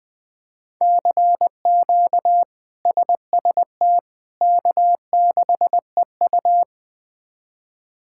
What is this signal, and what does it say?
Morse code, heard as a keyed tone, about 21 wpm, 700 Hz. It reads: CQ SST K6EU
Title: K1USN SST QSO Practice - 20wpm